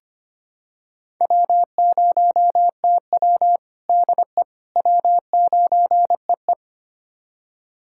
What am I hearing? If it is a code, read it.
Morse code, heard as a keyed tone, about 25 wpm, 700 Hz. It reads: W0TW DE W9EE
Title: USA Callsign DE Callsign - 4 Characters - 25wpm